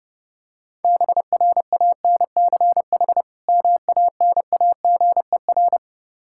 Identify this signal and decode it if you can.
Morse code, heard as a keyed tone, about 30 wpm, 700 Hz. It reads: BRANCH MANAGER
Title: Practice QSO - Exchange 3 - More Info - 30wpm